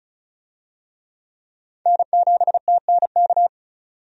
Morse code, heard as a keyed tone, about 34 wpm, 700 Hz. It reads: N7TNK
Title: USA Call Signs - 5 Characters - 35wpm